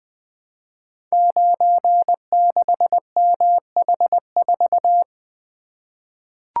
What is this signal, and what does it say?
Morse code, heard as a keyed tone, about 20 wpm, 700 Hz. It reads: 96MH4
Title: Code Groups - 5 Letter:Numbers - 20wpm